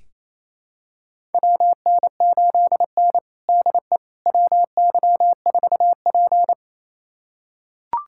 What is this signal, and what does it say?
Morse code, heard as a keyed tone, about 28 wpm, 700 Hz. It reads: WN8N DE WY4P
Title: USA Callsign DE Callsign - 4 Characters - 28wpm